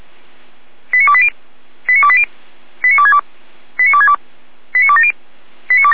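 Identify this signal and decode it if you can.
Five-tone selective call (ZVEI), heard as quick runs of five tones, tones of about 70 ms. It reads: 89280 89280 89262 89262 89270
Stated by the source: ZVEI five-tone selective call 89303, received on an SDR.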